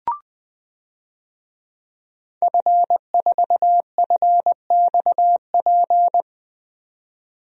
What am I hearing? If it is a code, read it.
Morse code, heard as a keyed tone, about 20 wpm, 700 Hz. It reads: F4FXP
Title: DXCC-100 Call Signs - 5 Characters - 20wpm